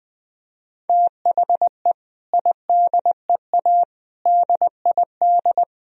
Morse code, heard as a keyed tone, about 20 wpm, 700 Hz. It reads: THE IDEA DID
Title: Sentences from Top 1000 Words - Review Part 1 - 20wpm